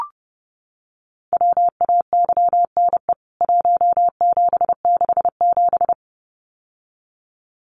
Morse code, heard as a keyed tone, about 30 wpm, 700 Hz. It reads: WAYNE 1767
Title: CWT Exchange 30wpm